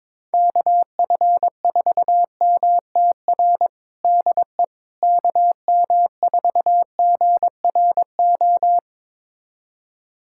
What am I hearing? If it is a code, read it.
Morse code, heard as a keyed tone, about 22 wpm, 700 Hz. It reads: KF4MTR DE KM4GRO
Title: USA Callsign DE Callsign 22wpm